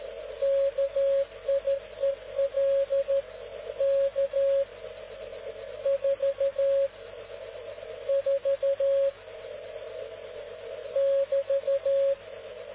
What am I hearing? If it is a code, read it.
Morse code, heard as a keyed tone, about 13 wpm, 550 Hz. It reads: KIEL K 4 4 <BT>